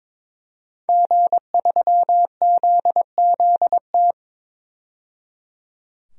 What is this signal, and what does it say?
Morse code, heard as a keyed tone, about 22 wpm, 700 Hz. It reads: G3ZZT
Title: DXCC-100 Call Signs - 5 Characters - 22wpm